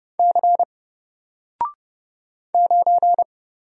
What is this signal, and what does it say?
Morse code, heard as a keyed tone, about 30 wpm, 700 Hz. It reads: C 9
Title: Single Letter-Number Alt 30wpm